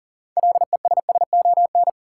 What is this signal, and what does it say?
Morse code, heard as a keyed tone, about 38 wpm, 700 Hz. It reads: LESSON
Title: Six Letter Words - 40wpm